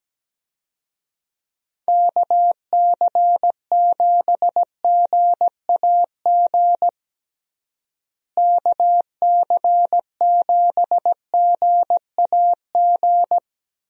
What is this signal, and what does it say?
Morse code, heard as a keyed tone, about 17 wpm, 700 Hz. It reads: KC7GAG KC7GAG
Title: USA Call Signs - 6 Characters - Repeated 2x - 17wpm